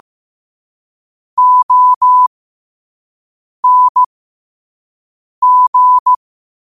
Morse code, heard as a keyed tone, about 15 wpm, spaced as for 5 wpm, 1000 Hz. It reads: ONG
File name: qotdfiles / 5wpm